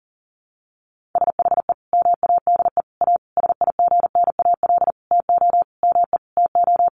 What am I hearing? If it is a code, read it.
Morse code, heard as a keyed tone, about 40 wpm, 700 Hz. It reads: SHE MADE A SIGNAL TO ME TO
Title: Sentences from Top 1400 Words 40wpm